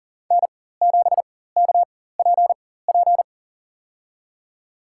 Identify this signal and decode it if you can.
Morse code, heard as a keyed tone, about 37 wpm, spaced as for 18 wpm, 700 Hz. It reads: N7KPP
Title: USA Call Signs - 4x Character Spacing - 40wpm